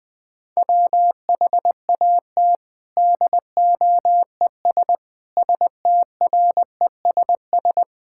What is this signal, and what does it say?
Morse code, heard as a keyed tone, about 20 wpm, 700 Hz. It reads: WHAT DOES STRESS
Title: Sentences from Top 1000 Words - Review Part 6 - 20wpm